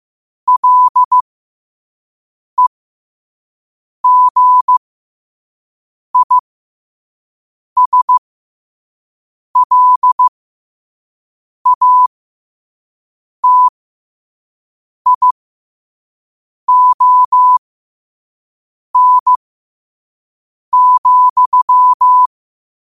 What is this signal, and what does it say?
Morse code, heard as a keyed tone, about 15 wpm, spaced as for 5 wpm, 1000 Hz. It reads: LEGISLATION,
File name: qotdfiles / 5wpm